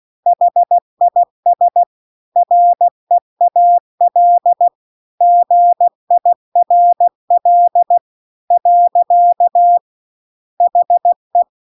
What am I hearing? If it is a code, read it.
Morse code, heard as a keyed tone, about 16 wpm, 700 Hz. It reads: HIS REAL GIRL . HE